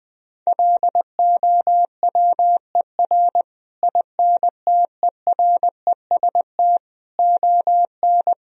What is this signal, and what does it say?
Morse code, heard as a keyed tone, about 20 wpm, 700 Hz. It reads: LOWER INTEREST ON
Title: Sentences from Top 1000 Words - Review Part 4 - 20wpm